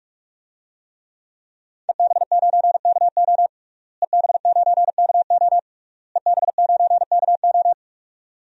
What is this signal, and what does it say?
Morse code, heard as a keyed tone, about 41 wpm, 700 Hz. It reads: EB9KO EB9KO EB9KO
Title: DXCC-100 Call Signs - 5 Characters - Repeated 3x - 45wpm